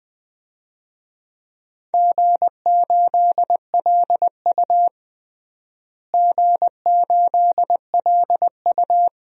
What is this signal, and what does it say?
Morse code, heard as a keyed tone, about 20 wpm, 700 Hz. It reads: G8LU G8LU
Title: DXCC-100 Call Signs - 4 Characters - Repeated 2x - 20wpm